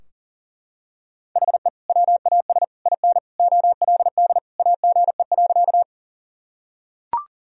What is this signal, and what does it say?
Morse code, heard as a keyed tone, about 40 wpm, 700 Hz. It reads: HE WAS IN OLD AGE.
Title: Sentences from Top 1000 Words - Review Part 3 - 40wpm